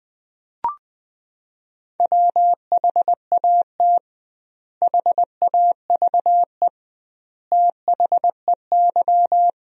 Morse code, heard as a keyed tone, about 21 wpm, 700 Hz. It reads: WHAT HAVE THEY
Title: 2x Word-Spacing - Sentences from Top 500 Words - 20wpm